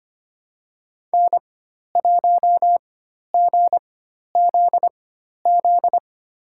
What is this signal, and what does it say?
Morse code, heard as a keyed tone, about 25 wpm, spaced as for 12 wpm, 700 Hz. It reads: N1GZZ
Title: USA Call Signs - 5 Characters - 4x Character Spacing - 25wpm